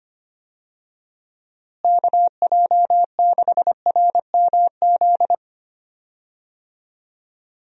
Morse code, heard as a keyed tone, about 25 wpm, 700 Hz. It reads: KJ6RMZ
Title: USA Call Signs - 6 Characters - Repeated 3x - 25wpm